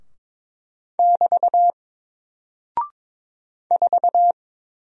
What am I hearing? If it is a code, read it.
Morse code, heard as a keyed tone, about 22 wpm, 700 Hz. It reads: <BT> 4
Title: Single Character (Letter:Number:Punctuation:Prosign) 22wpm